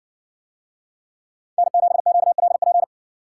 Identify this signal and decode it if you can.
Morse code, heard as a keyed tone, about 60 wpm, 700 Hz. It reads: N7QLP